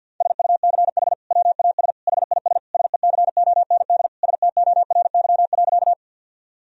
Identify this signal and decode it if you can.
Morse code, heard as a keyed tone, about 52 wpm, 700 Hz. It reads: SUCH WAS HIS SECOND STORY.
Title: Sentences from Top 1000 Words - Review Part 1 - 50wpm